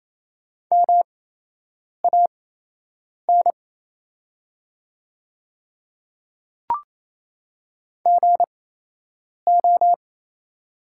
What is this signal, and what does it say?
Morse code, heard as a keyed tone, about 28 wpm, spaced as for 8 wpm, 700 Hz. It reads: MAN GO
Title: 8x Character Spacing - Top 200 Words - 28wpm